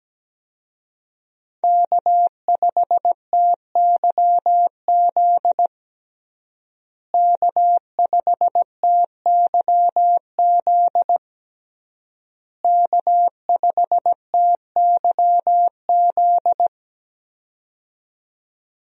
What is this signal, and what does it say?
Morse code, heard as a keyed tone, about 17 wpm, 700 Hz. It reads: K5TYZ K5TYZ K5TYZ
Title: USA Call Signs - 5 Characters - Repeated 3x - 17wpm